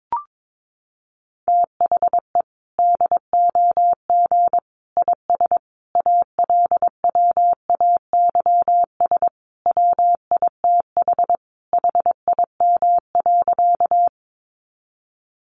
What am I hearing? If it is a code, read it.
Morse code, heard as a keyed tone, about 22 wpm, 700 Hz. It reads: THE DOG IS ALWAYS WITH HIM.
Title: Sentences from Top 400 Words 22wpm